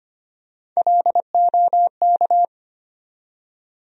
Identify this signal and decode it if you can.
Morse code, heard as a keyed tone, about 25 wpm, 700 Hz. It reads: LOK
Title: USA Call Sign Postfixes - 3 Letters 25wpm